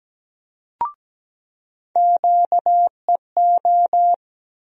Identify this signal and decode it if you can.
Morse code, heard as a keyed tone, about 17 wpm, 700 Hz. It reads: QEO
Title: USA Call Sign Postfixes - 3 Letters 17wpm